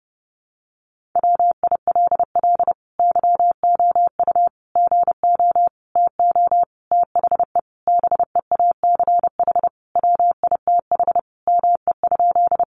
Morse code, heard as a keyed tone, about 28 wpm, 700 Hz. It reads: WILL YOU GO TO THE BEACH WITH ME?
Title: Sentences from Top 1500 Words 30wpm